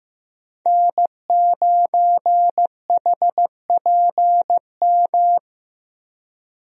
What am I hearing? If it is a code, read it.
Morse code, heard as a keyed tone, about 15 wpm, 700 Hz. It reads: N9HPM
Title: USA Call Signs - 5 Characters - Repeated 3x - 15wpm